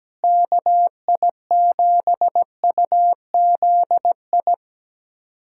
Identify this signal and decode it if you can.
Morse code, heard as a keyed tone, about 17 wpm, 700 Hz. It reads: KI7UZI
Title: USA Callsign DE Callsign - 6 Characters - 17wpm